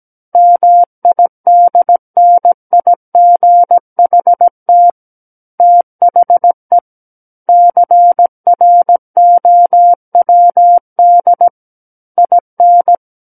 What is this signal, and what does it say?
Morse code, heard as a keyed tone, about 17 wpm, 700 Hz. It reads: MIDNIGHT THE CROWD IN
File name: Book One / 08-Chapter 8 - 17 WPM